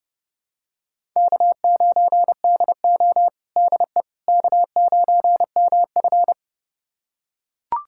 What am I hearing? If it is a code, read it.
Morse code, heard as a keyed tone, about 31 wpm, 700 Hz. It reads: K9DO DE K9MF
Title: USA Callsign DE Callsign - 4 Characters - 30wpm